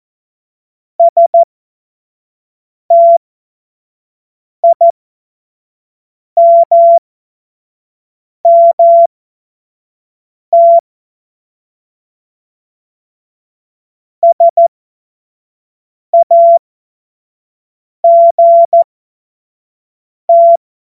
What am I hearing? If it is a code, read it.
Morse code, heard as a keyed tone, about 14 wpm, spaced as for 5 wpm, 680 Hz. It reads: STIMMT SAGT
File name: Morsekurs / Witze / Witz 21. Februar 2023 25 F70